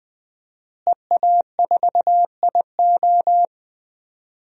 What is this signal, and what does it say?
Morse code, heard as a keyed tone, about 20 wpm, 700 Hz. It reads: EA4IO
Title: DXCC-100 Call Signs - 5 Characters - Repeated 2x - 20wpm